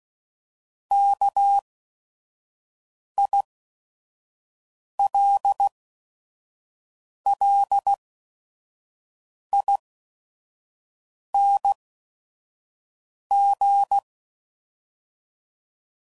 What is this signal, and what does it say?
Morse code, heard as a keyed tone, about 16 wpm, spaced as for 5 wpm, 800 Hz. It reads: KILLING
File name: Suffixes / Special ING's / VerySlowToFaster ING's